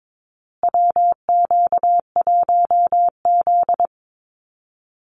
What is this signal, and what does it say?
Morse code, heard as a keyed tone, about 22 wpm, 700 Hz. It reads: WQ1Z
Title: USA Call Signs Repeated 2x 22wpm